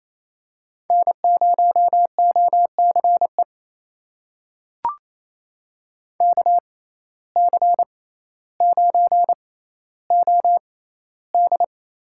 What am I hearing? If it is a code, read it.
Morse code, heard as a keyed tone, about 28 wpm, spaced as for 14 wpm, 700 Hz. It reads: N0OCE KC9OD
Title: USA Call Signs - 5 Characters - 6x Character Spacing - 28wpm